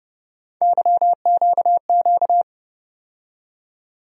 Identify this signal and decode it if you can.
Morse code, heard as a keyed tone, about 29 wpm, 700 Hz. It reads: YQQ
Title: USA Call Sign Postfixes - 3 Letters 30wpm